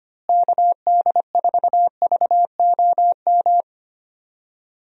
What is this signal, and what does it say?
Morse code, heard as a keyed tone, about 25 wpm, 700 Hz. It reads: KD4VOM
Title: USA Call Signs - 6 Characters - Repeated 3x - 25wpm